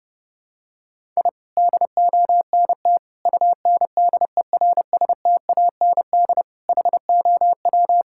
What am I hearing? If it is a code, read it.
Morse code, heard as a keyed tone, about 30 wpm, 700 Hz. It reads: I DONT UNDERSTAND HOW
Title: Sentences from Top 1000 Words - Review Part 5 - 30wpm